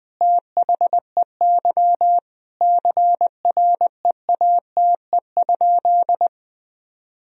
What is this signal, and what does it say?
Morse code, heard as a keyed tone, about 20 wpm, 700 Hz. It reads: THEY CREATE?
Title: Sentences from Top 1000 Words - Review Part 3 - 20wpm